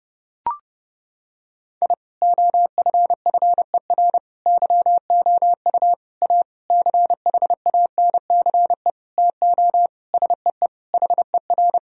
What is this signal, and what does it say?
Morse code, heard as a keyed tone, about 30 wpm, 700 Hz. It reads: I OFFER YOU A CHANCE TO SEE HER
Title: Sentences from Top 1000 Words - Review Part 4 - 30wpm